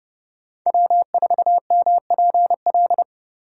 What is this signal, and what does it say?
Morse code, heard as a keyed tone, about 39 wpm, 700 Hz. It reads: W4MPL